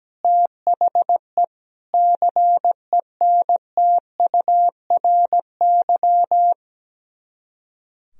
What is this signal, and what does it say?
Morse code, heard as a keyed tone, about 17 wpm, 700 Hz. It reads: THE CENTURY
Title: Speed-Racing - Sets of 5 Words 17wpm